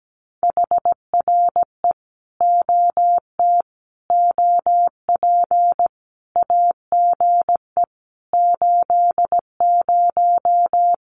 Morse code, heard as a keyed tone, about 17 wpm, 700 Hz. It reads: HRE OT OP AGE 80
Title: Common QSO Phrases 17wpm